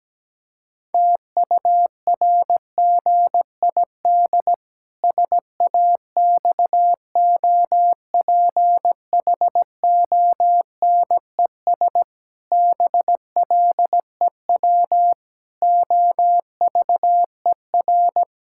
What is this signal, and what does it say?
Morse code, heard as a keyed tone, about 17 wpm, 700 Hz. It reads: TURGID SAXOPHONES BLEW OVER
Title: Pangram Challenge v2 - 17wpm